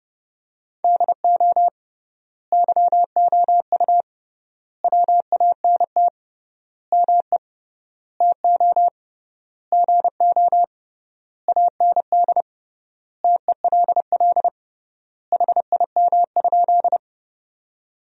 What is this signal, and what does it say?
Morse code, heard as a keyed tone, about 32 wpm, 700 Hz. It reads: DO YOU WANT ME TO GO AND TELL HIM?
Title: 3x Word-Spacing - Sentences from Top 200 Words - 30wpm